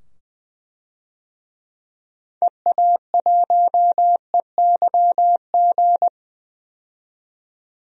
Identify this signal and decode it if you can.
Morse code, heard as a keyed tone, about 20 wpm, 700 Hz. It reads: EA1EYG
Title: DXCC-100 Call Signs - 6 Characters - 20wpm